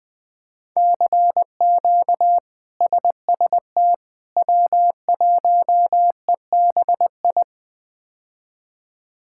Morse code, heard as a keyed tone, about 20 wpm, 700 Hz. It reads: CQ SST W1EBI
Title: K1USN SST QSO Practice - 20wpm